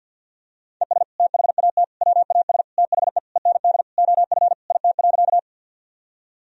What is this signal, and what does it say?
Morse code, heard as a keyed tone, about 48 wpm, 700 Hz. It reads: ES THAT WAS THE END OF IT.
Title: Sentences from Top 200 Words and Abbreviations 50wpm